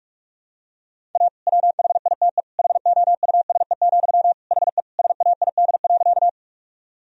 Morse code, heard as a keyed tone, about 47 wpm, 700 Hz. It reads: A WHITE HOUSE, HE SAID.
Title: Sentences from Top 1000 Words - Review Part 1 - 45wpm